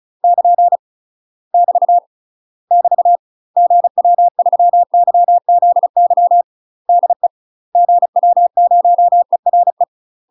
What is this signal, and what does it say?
Morse code, heard as a keyed tone, about 35 wpm, 700 Hz. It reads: <KN> <BT> <BT> GW3YZY DE GW0ERE
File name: UK MASTER QSOs TRAINING / 35WPM / UK MASTER QSOs 0004